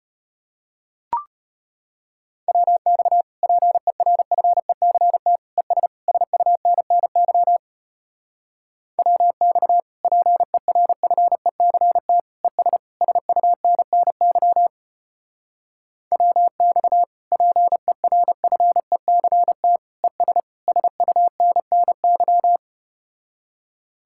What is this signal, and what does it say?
Morse code, heard as a keyed tone, about 33 wpm, 700 Hz. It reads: WX PERFECT ES SUNNY WX PERFECT ES SUNNY WX PERFECT ES SUNNY